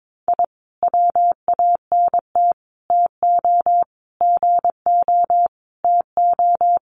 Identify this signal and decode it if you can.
Morse code, heard as a keyed tone, about 22 wpm, 700 Hz. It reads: I WANT TO GO TO
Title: Sets of 5 Words Encore 22wpm